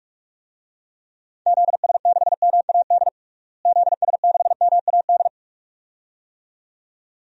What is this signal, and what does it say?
Morse code, heard as a keyed tone, about 46 wpm, 700 Hz. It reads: ZS6MAD ZS6MAD
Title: DXCC-100 Call Signs - 6 Characters - Repeated 2x - 45wpm